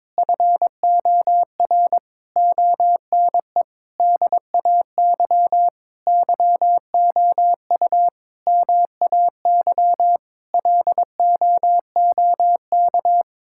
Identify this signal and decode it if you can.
Morse code, heard as a keyed tone, about 22 wpm, 700 Hz. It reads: FOR ONE DAY YOU MAY LOOK
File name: Difficult Sentences with Hints - 22wpm